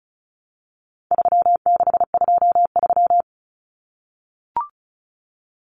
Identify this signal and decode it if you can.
Morse code, heard as a keyed tone, about 34 wpm, 700 Hz. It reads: 3623
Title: Numbers - 4-Digit 35wpm